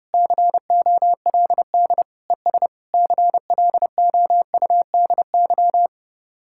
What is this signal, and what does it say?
Morse code, heard as a keyed tone, about 33 wpm, 700 Hz. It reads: COLD ES CLOUDY